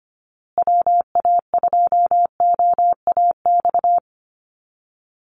Morse code, heard as a keyed tone, about 25 wpm, 700 Hz. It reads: WA2OAX
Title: Call Signs Repeated 2x - 25wpm